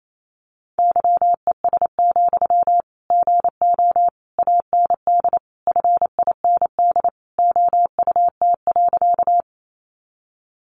Morse code, heard as a keyed tone, about 28 wpm, 700 Hz. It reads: YES, GO AND FIND OUT.
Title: Sentences from Top 500 Words 28wpm